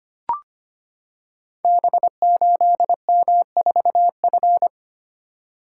Morse code, heard as a keyed tone, about 25 wpm, 700 Hz. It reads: B8M4F
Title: Code Groups - 5 Letter:Numbers - 25wpm